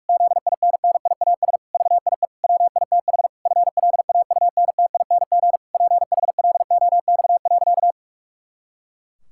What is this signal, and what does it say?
Morse code, heard as a keyed tone, about 46 wpm, 700 Hz. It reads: ZINNIAS VIE WITH FLAUNTING PHLOX.
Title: Pangram Challenge v2 - 45wpm